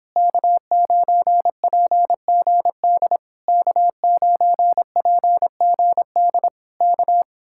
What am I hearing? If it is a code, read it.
Morse code, heard as a keyed tone, about 26 wpm, 700 Hz. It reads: K9PGD K9PGD K